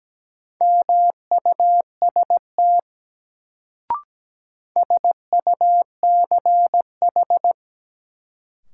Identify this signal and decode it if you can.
Morse code, heard as a keyed tone, about 17 wpm, 700 Hz. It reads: MUST SUCH
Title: Sentences from Top 200 Words - New Word Drill - 17wpm